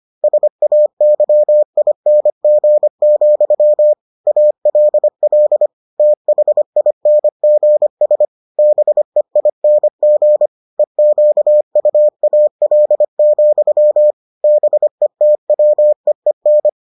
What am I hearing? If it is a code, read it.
Morse code, heard as a keyed tone, about 25 wpm, 600 Hz. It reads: SAYING, ALL THINGS BEING EQUAL, BETWEEN